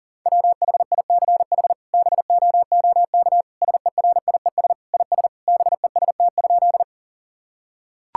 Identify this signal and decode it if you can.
Morse code, heard as a keyed tone, about 40 wpm, 700 Hz. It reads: WHICH BOOK SERIES IS BEST?
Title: Sentences from Top 1000 Words - Review Part 5 - 40wpm